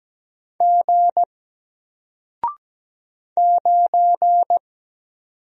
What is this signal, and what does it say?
Morse code, heard as a keyed tone, about 17 wpm, 700 Hz. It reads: G 9
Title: Single Character (Letter:Number:Punctuation:Prosign) 17wpm